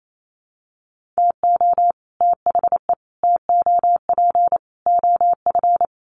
Morse code, heard as a keyed tone, about 28 wpm, 700 Hz. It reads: TO THE TOP OF
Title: Sets of 4 Words 28wpm